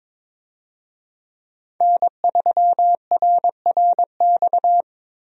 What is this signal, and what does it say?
Morse code, heard as a keyed tone, about 22 wpm, 700 Hz. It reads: N3RRX
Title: USA Call Signs - 5 Characters - Repeated 2x - 22wpm